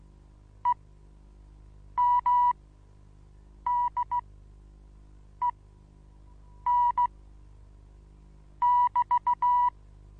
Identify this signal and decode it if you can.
Morse code, heard as a keyed tone, about 16 wpm, spaced as for 6 wpm, 980 Hz. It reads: EMDEN<BT>
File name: Onlinelehrgang / morsen / DJ4UF Morsen25